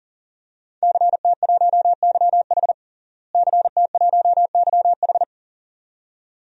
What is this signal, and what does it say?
Morse code, heard as a keyed tone, about 40 wpm, 690 Hz. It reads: CT1YH CT1YH
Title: DXCC-100 Call Signs - 5 Characters - Repeated 2x - 40wpm